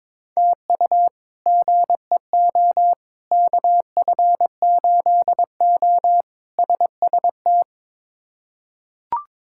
Morse code, heard as a keyed tone, about 22 wpm, 700 Hz. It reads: TU GEO KF8O SST
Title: K1USN SST QSO Practice - 22wpm